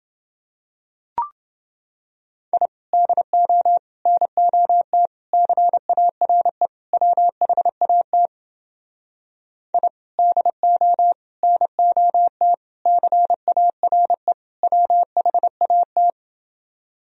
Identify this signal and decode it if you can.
Morse code, heard as a keyed tone, about 30 wpm, 700 Hz. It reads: I DO NOT CARE WHAT I DO NOT CARE WHAT